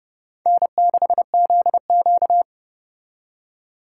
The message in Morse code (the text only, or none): N6ZQ